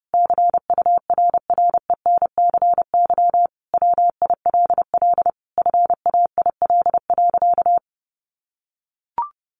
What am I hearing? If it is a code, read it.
Morse code, heard as a keyed tone, about 30 wpm, 700 Hz. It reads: CURRENCY WILL FAIL.
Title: Sentences from Top 1500 Words 30wpm